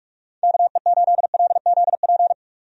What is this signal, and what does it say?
Morse code, heard as a keyed tone, about 46 wpm, 700 Hz. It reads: KE8LZP
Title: USA Call Signs - 6 Characters - 45wpm